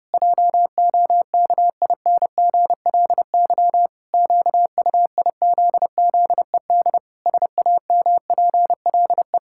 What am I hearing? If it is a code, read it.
Morse code, heard as a keyed tone, about 32 wpm, 700 Hz. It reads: JOKINGLY QUIZZED SAMPLE
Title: Pangram Challenge v2 - 30wpm